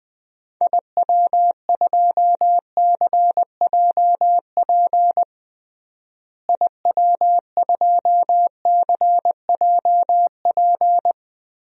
Morse code, heard as a keyed tone, about 21 wpm, 700 Hz. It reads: IW2CJP IW2CJP
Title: DXCC-100 Call Signs - 6 Characters - Repeated 3x - 20wpm